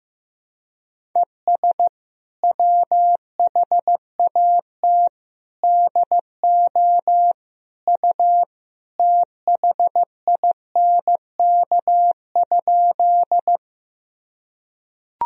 Morse code, heard as a keyed tone, about 16 wpm, 700 Hz. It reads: ES WHAT DO U THINK?
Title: Sentences from Top 200 Words and Abbreviations 15wpm